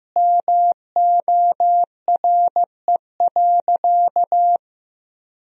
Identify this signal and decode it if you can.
Morse code, heard as a keyed tone, about 15 wpm, 700 Hz. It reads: MORE.
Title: Sentences from Top 1000 Words - Review Part 4 - 15wpm